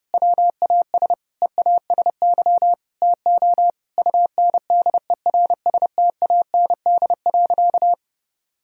Morse code, heard as a keyed tone, about 30 wpm, 700 Hz. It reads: WAS EASY TO UNDERSTAND.
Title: Sentences from Top 1000 Words - Review Part 3 - 30wpm